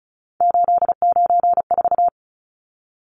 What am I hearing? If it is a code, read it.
Morse code, heard as a keyed tone, about 34 wpm, 700 Hz. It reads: 894
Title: Numbers - 3-Digit 35wpm